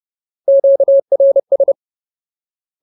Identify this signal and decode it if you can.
Morse code, heard as a keyed tone, about 30 wpm, 550 Hz. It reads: QRS
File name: Ultimate Word List / 30WPM Ultimate